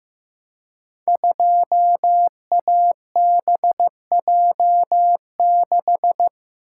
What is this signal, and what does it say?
Morse code, heard as a keyed tone, about 15 wpm, 700 Hz. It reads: 2ABJ6
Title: Code Groups - 5 Letter:Numbers - 15wpm